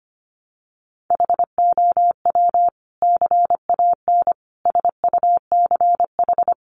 Morse code, heard as a keyed tone, about 25 wpm, 700 Hz. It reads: HOW CAN SUCH
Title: Sentences from Top 1100 Words 25wpm